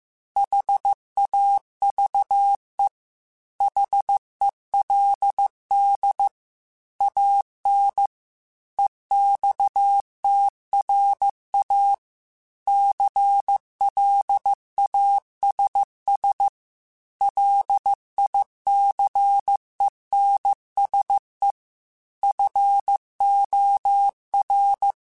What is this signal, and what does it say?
Morse code, heard as a keyed tone, about 15 wpm, 800 Hz. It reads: HAVE HELD AN EXTRA CLASS LICENSE FOR